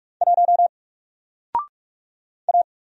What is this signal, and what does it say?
Morse code, heard as a keyed tone, about 44 wpm, 700 Hz. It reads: 1 A
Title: Single Letter-Number Alt 45wpm